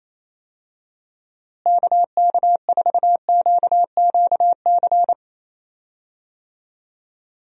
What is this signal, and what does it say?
Morse code, heard as a keyed tone, about 28 wpm, 700 Hz. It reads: KK4QQC
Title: USA Call Signs - 6 Characters - Repeated 3x - 28wpm